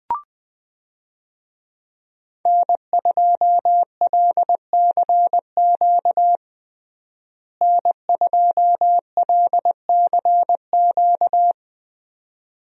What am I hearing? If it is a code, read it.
Morse code, heard as a keyed tone, about 20 wpm, 700 Hz. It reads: N2LCQ N2LCQ
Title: USA Call Signs - 5 Characters - Repeated 2x - 20wpm